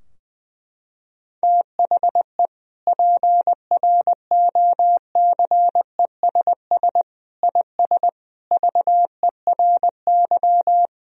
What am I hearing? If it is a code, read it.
Morse code, heard as a keyed tone, about 20 wpm, 700 Hz. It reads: THE PROCESS IS VERY
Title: Sentences from Top 1000 Words - Review Part 3 - 20wpm